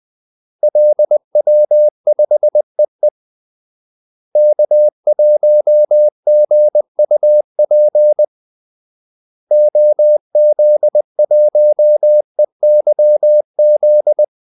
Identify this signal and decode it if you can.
Morse code, heard as a keyed tone, about 20 wpm, 600 Hz. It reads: LW5EE K1GUP OZ1EYZ
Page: Kaldesignaler 20 wpm